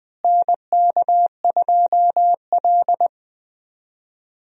Morse code, heard as a keyed tone, about 20 wpm, 700 Hz. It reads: NK2L
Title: USA Call Signs - 4 Characters - Repeated 3x - 20wpm